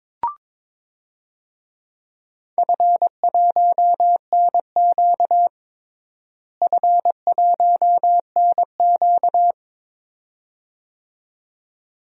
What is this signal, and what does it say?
Morse code, heard as a keyed tone, about 22 wpm, 700 Hz. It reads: F1NQ F1NQ
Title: DXCC-100 Call Signs - 4 Characters - Repeated 2x - 22wpm